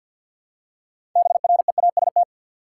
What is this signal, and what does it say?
Morse code, heard as a keyed tone, about 50 wpm, 700 Hz. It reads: BREAST
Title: Six Letter Words - 50wpm